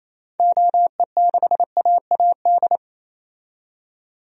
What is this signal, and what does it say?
Morse code, heard as a keyed tone, about 28 wpm, 700 Hz. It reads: OE6AAD
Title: DXCC-100 Call Signs - 6 Characters - 28wpm